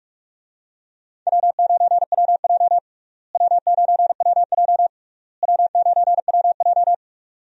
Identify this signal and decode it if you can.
Morse code, heard as a keyed tone, about 41 wpm, 700 Hz. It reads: W9WJ W9WJ W9WJ
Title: USA Call Signs - 4 Characters - Repeated 3x - 45wpm